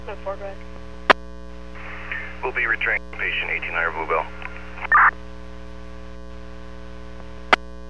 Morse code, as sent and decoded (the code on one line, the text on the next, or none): none